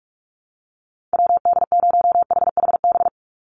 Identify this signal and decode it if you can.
Morse code, heard as a keyed tone, about 44 wpm, 700 Hz. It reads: WD0HHB
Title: USA Call Signs 45wpm